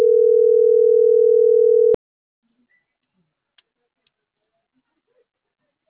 Ringback tone (North America), 440 + 480 Hz.